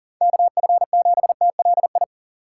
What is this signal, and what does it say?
Morse code, heard as a keyed tone, about 48 wpm, 700 Hz. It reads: KF7TLI
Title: Speed Racing - USA 2x3 Call Signs 40wpm